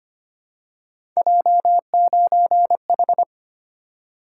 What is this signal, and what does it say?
Morse code, heard as a keyed tone, about 25 wpm, 700 Hz. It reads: J9H
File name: Three Letter-Number Combinations - 25wpm